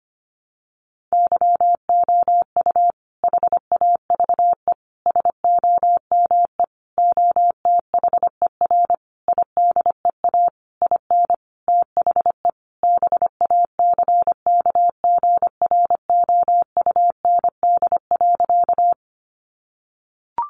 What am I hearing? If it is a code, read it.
Morse code, heard as a keyed tone, about 25 wpm, 700 Hz. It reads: YOU HAVE SOME OTHER IDEA IN THE BACKGROUND.
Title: Sentences from Top 1300 Words 25wpm